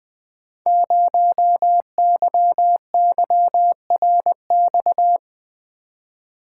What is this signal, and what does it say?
Morse code, heard as a keyed tone, about 20 wpm, 700 Hz. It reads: 0YYRX
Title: Code Groups - 5 Letter:Numbers - 20wpm